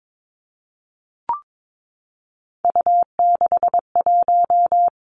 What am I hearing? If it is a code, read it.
Morse code, heard as a keyed tone, about 22 wpm, 700 Hz. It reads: U61
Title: Three Challenging Letter-Number Combinations 22wpm